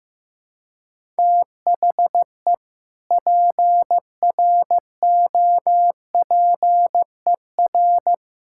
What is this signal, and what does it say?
Morse code, heard as a keyed tone, about 16 wpm, 700 Hz. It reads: THE PROPER
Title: Sentences from Top 1600 Words 15wpm